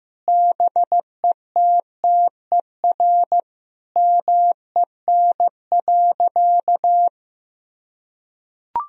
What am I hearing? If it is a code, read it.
Morse code, heard as a keyed tone, about 15 wpm, 700 Hz. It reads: BETTER MEN.
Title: Sentences from Top 1000 Words - Review Part 3 - 15wpm